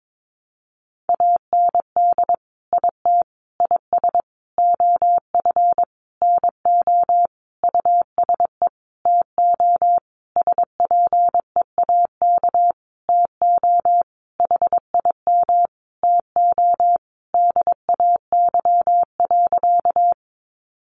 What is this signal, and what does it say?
Morse code, heard as a keyed tone, about 22 wpm, 700 Hz. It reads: AND IT IS OF NO USE TO SPEAK TO HIM TO DAY.
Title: Sentences from Top 1100 Words 22wpm